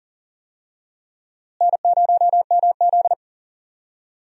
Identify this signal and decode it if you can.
Morse code, heard as a keyed tone, about 40 wpm, 700 Hz. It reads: N0MZ
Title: USA Call Signs - 4 Characters - Repeated 2x - 40wpm